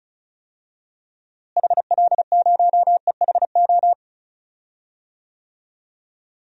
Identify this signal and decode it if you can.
Morse code, heard as a keyed tone, about 34 wpm, 700 Hz. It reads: HL0EHO
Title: DXCC-100 Call Signs - 6 Characters - Repeated 3x - 35wpm